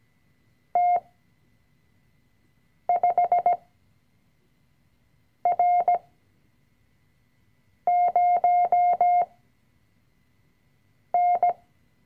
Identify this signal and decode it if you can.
Morse code, heard as a keyed tone, about 17 wpm, spaced as for 4 wpm, 710 Hz. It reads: T5R0N
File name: CW-Online-Kurs / Anfänger / Lektion 005